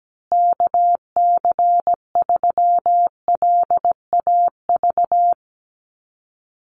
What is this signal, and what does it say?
Morse code, heard as a keyed tone, about 17 wpm, 700 Hz. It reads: KC3LAV
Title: USA Call Signs Repeated 3x 17wpm